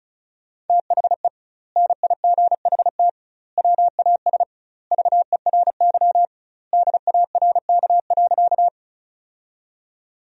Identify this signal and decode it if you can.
Morse code, heard as a keyed tone, about 36 wpm, 700 Hz. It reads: THE NIGHT WAS VERY DARK.
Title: 2x Word-Spacing - Sentences from Top 500 Words - 35wpm